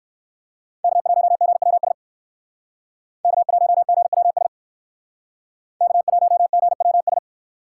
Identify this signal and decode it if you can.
Morse code, heard as a keyed tone, about 59 wpm, 700 Hz. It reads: K1GWS K1GWS K1GWS